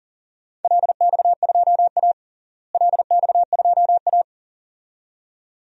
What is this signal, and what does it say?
Morse code, heard as a keyed tone, about 37 wpm, 700 Hz. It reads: LX2A LX2A
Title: DXCC-100 Call Signs - 4 Characters - Repeated 2x - 40wpm